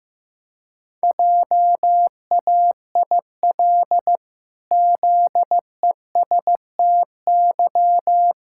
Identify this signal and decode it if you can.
Morse code, heard as a keyed tone, about 15 wpm, 700 Hz. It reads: JAIL ZESTY
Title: Pangram Challenge v2 - 15wpm